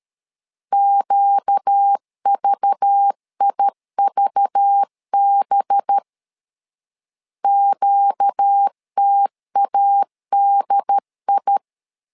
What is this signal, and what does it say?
Morse code, heard as a keyed tone, about 13 wpm, 800 Hz. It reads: QVIVB QTADI
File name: LEZIONI / 5 LBVC / 5 LBV 12wpm 1855